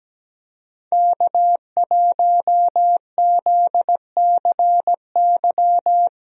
Morse code, heard as a keyed tone, about 17 wpm, 700 Hz. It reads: K1ZCY
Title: USA Call Signs - 5 Characters - Repeated 3x - 17wpm